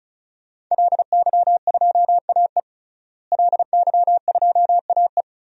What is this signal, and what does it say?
Morse code, heard as a keyed tone, about 36 wpm, 700 Hz. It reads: LY2AE LY2AE
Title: DXCC-100 Call Signs - 5 Characters - Repeated 2x - 35wpm